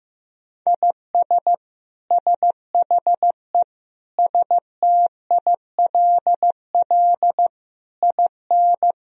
Morse code, heard as a keyed tone, about 15 wpm, 700 Hz. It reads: IS SHE STILL IN
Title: Sentences from Top 1000 Words - Review Part 2 - 15wpm